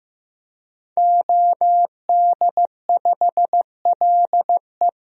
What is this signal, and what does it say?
Morse code, heard as a keyed tone, about 15 wpm, 700 Hz. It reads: OD5LE
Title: DXCC-100 Call Signs - 5 Characters - Repeated 2x - 15wpm